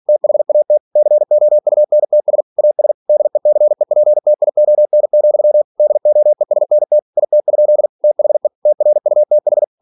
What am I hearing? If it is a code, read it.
Morse code, heard as a keyed tone, about 46 wpm, 600 Hz. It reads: THAT COUNTS AS DECEPTION, DOESNT IT? THE TRUTH
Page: Never 47wpm